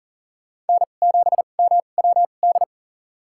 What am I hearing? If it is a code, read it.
Morse code, heard as a keyed tone, about 40 wpm, spaced as for 29 wpm, 700 Hz. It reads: N7MWD
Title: USA Call Signs - 5 Characters - 2x Character Spacing - 40wpm